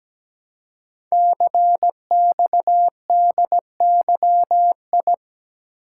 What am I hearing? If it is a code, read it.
Morse code, heard as a keyed tone, about 17 wpm, 700 Hz. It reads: CXDYI
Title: Code Groups - 5 Letters - 17wpm